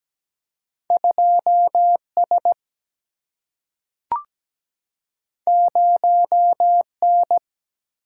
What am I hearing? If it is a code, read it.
Morse code, heard as a keyed tone, about 17 wpm, 700 Hz. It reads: 2S 0N
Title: Single-Letter Single-Number 17wpm